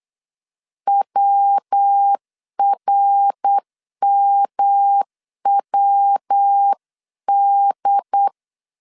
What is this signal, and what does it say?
Morse code, heard as a keyed tone, about 8 wpm, 800 Hz. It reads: WRMWD